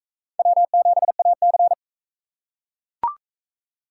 Morse code, heard as a keyed tone, about 40 wpm, 700 Hz. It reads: W7AC